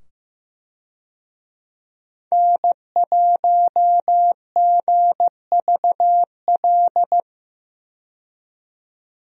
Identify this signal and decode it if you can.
Morse code, heard as a keyed tone, about 15 wpm, 700 Hz. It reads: N1GVL
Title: USA Call Signs - 5 Characters - 15wpm